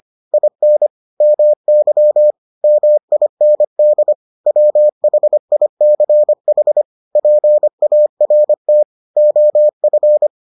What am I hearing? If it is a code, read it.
Morse code, heard as a keyed tone, about 25 wpm, 600 Hz. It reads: IN MY MIND WHICH PART OF